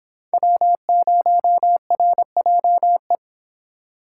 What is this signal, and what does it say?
Morse code, heard as a keyed tone, about 25 wpm, 700 Hz. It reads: W0RJE